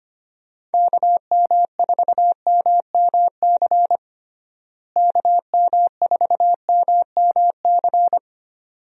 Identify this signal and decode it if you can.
Morse code, heard as a keyed tone, about 25 wpm, 700 Hz. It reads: KM4MMC KM4MMC
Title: USA Call Signs - 6 Characters - Repeated 2x - 25wpm